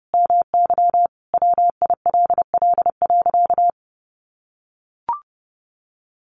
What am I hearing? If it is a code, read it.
Morse code, heard as a keyed tone, about 30 wpm, 700 Hz. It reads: MY WILL.
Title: Sentences from Top 1500 Words 30wpm